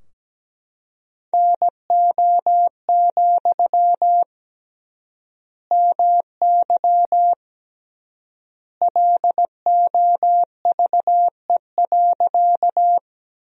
Morse code, heard as a keyed tone, about 17 wpm, 700 Hz. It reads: NO, MY LOVE.
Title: 3x Word-Spacing - Sentences from Top 400 Words - 17wpm